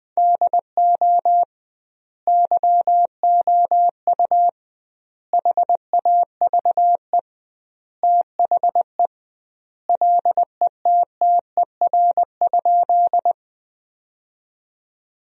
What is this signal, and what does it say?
Morse code, heard as a keyed tone, about 20 wpm, 700 Hz. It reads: DO YOU HAVE THE LETTER?
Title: 2x Word-Spacing - Sentences from Top 300 Words - 20wpm